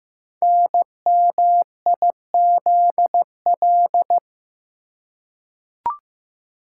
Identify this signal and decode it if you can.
Morse code, heard as a keyed tone, about 15 wpm, 700 Hz. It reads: NMIZL
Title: Code Groups - 5 Letters - 15wpm